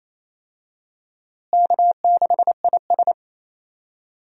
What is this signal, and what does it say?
Morse code, heard as a keyed tone, about 28 wpm, 700 Hz. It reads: K6IS
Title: USA Call Signs - 4 Characters - Repeated 3x - 28wpm